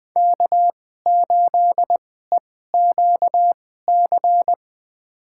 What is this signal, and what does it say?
Morse code, heard as a keyed tone, about 20 wpm, spaced as for 14 wpm, 700 Hz. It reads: K8EQC
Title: USA Call Signs - 5 Characters - 2x Character Spacing - 20wpm